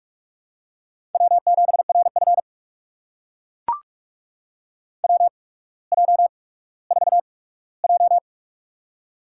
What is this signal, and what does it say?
Morse code, heard as a keyed tone, about 44 wpm, spaced as for 19 wpm, 700 Hz. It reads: W7RF WJ4J
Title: USA Call Signs - 4 Characters - 8x Character Spacing - 45wpm